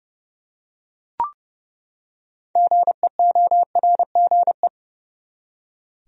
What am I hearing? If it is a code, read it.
Morse code, heard as a keyed tone, about 31 wpm, 700 Hz. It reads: GEORGE
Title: USA Common Names - 30wpm